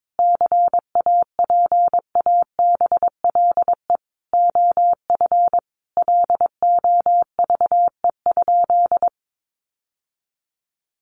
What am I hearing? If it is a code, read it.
Morse code, heard as a keyed tone, about 22 wpm, 700 Hz. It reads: CAPABLE OF LOVE?
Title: Sentences from Top 1200 Words 22wpm